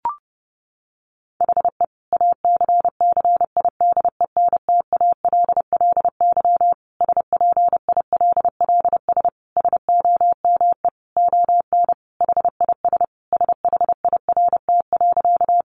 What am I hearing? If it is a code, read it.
Morse code, heard as a keyed tone, about 30 wpm, 700 Hz. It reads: HE ACCIDENTALLY SPILLS SOME ON HIS SHIRT.
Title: Additional Easy Phrases 30wpm